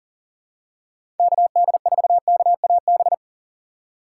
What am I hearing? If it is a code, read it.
Morse code, heard as a keyed tone, about 40 wpm, 700 Hz. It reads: KD4KAB
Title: USA Call Signs - 4x Character Spacing - 40wpm